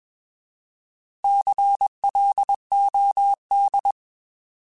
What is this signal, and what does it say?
Morse code, heard as a keyed tone, about 21 wpm, 800 Hz. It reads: CLOD
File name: Alphabetical / Alphabetical mp3s / Cs @20